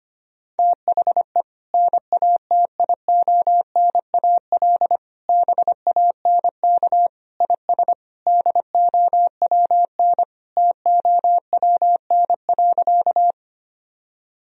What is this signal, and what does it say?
Morse code, heard as a keyed tone, about 25 wpm, 700 Hz. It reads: THE NATIONAL BANK IS DOWN TOWN.
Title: Sentences from Top 1000 Words - Review Part 4 - 25wpm